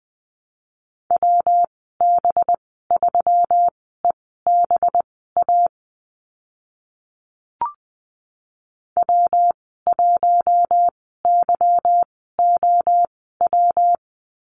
Morse code, heard as a keyed tone, about 20 wpm, spaced as for 14 wpm, 700 Hz. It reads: WB3EBA W1YOW
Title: 193 Lesson 31g - Letter v - Call Signs 20wpm